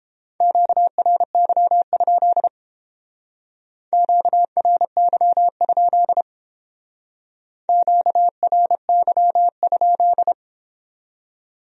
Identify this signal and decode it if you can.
Morse code, heard as a keyed tone, about 29 wpm, 700 Hz. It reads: QRY? QRY? QRY?